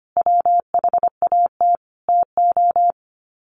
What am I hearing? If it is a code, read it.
Morse code, heard as a keyed tone, about 25 wpm, 700 Hz. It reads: WHAT TO
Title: Sets of 5 Words 25wpm